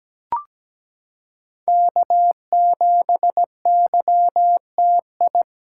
Morse code, heard as a keyed tone, about 17 wpm, 700 Hz. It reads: K7YTI
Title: USA Callsign DE Callsign - 5 Characters - 17wpm